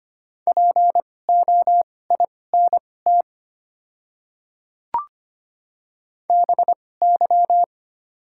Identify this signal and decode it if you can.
Morse code, heard as a keyed tone, about 25 wpm, spaced as for 18 wpm, 690 Hz. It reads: POINT BY
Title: 2x Character Spacing - Top 300 Words - 25wpm